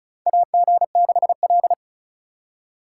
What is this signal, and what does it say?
Morse code, heard as a keyed tone, about 34 wpm, 700 Hz. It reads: AG6L
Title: USA Call Signs - 4 Characters - 3x Character Spacing - 35wpm